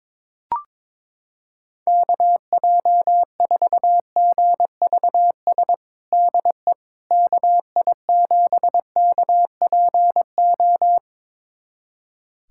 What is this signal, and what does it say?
Morse code, heard as a keyed tone, about 22 wpm, 700 Hz. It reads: KJ4GVS DE KI7KPO
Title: USA Callsign DE Callsign - 6 Characters - 22wpm